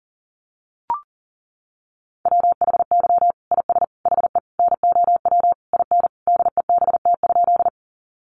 Morse code, heard as a keyed tone, about 40 wpm, 700 Hz. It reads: WHY IS HE NOW IN DEBT?
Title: Sentences from Top 1300 Words 40wpm